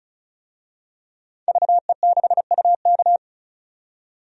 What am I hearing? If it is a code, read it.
Morse code, heard as a keyed tone, about 36 wpm, 700 Hz. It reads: VE6UK
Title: DXCC-100 Call Signs - 5 Characters - 35wpm